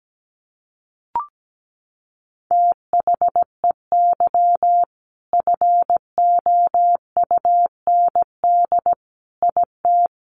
Morse code, heard as a keyed tone, about 17 wpm, 700 Hz. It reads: THEY FOUND IT
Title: Sentences from Top 1400 Words 17wpm